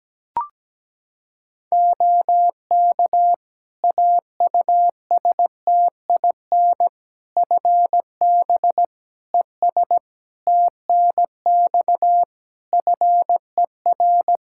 Morse code, heard as a keyed tone, about 17 wpm, 700 Hz. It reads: OK AUSTIN FB ES TNX FER
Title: Practice QSO - Exchange 2 - Info - 17wpm